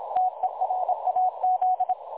Morse code, heard as a keyed tone, about 26 wpm, 740 Hz. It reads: TERAZ
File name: dlazaawansowanych / nagraniazpasma / websdr recording 3565.0kHz